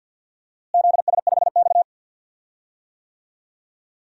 Morse code, heard as a keyed tone, about 52 wpm, 700 Hz. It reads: ZS5X
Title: DXCC-100 Call Signs - 4 Characters - Repeated 3x - 50wpm